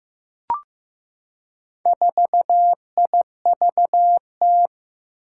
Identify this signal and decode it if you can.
Morse code, heard as a keyed tone, about 15 wpm, 700 Hz. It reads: 4IVT
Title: Four Challenging Letter-Number Combinations 15wpm